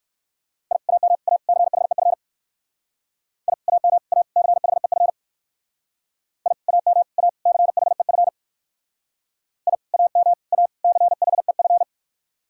Morse code, heard as a keyed tone, about 59 wpm, 700 Hz. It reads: I AM A CHEF I AM A CHEF I AM A CHEF I AM A CHEF